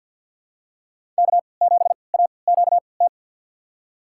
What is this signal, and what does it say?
Morse code, heard as a keyed tone, about 52 wpm, spaced as for 29 wpm, 700 Hz. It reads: K7AQT
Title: USA Call Signs - 5 Characters - 3x Character Spacing - 50wpm